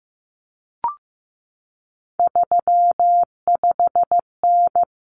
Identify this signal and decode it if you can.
Morse code, heard as a keyed tone, about 15 wpm, 700 Hz. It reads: 35N
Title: Common Cut Numbers - 3-Digit 15wpm